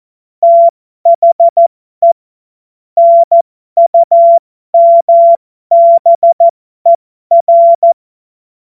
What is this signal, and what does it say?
Morse code, heard as a keyed tone, about 14 wpm, spaced as for 12 wpm, 680 Hz. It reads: THE NUMBER
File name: Morsekurs / Witze